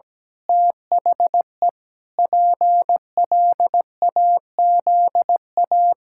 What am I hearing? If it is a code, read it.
Morse code, heard as a keyed tone, about 17 wpm, 700 Hz. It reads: THE PLAZA
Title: Pangram Challenge v2 - 17wpm